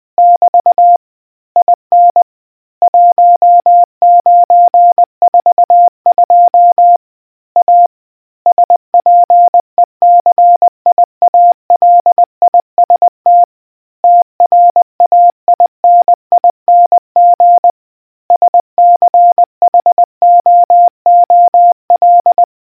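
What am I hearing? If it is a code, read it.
Morse code, heard as a keyed tone, about 20 wpm, 700 Hz. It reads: <BT> IN 1942 A SPECIALIST TRAINING SCHOOL
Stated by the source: Morse quiz question from the RNZN Communicators Association.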